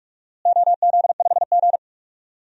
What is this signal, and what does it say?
Morse code, heard as a keyed tone, about 46 wpm, 690 Hz. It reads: OZ5G
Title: DXCC-100 Call Signs - 4 Characters - Repeated 3x - 45wpm